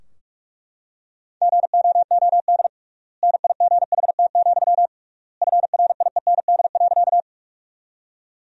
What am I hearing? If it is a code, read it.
Morse code, heard as a keyed tone, about 43 wpm, 700 Hz. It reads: GOOD NIGHT, FRIEND.
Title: 3x Word-Spacing - Sentences from Top 300 Words - 45wpm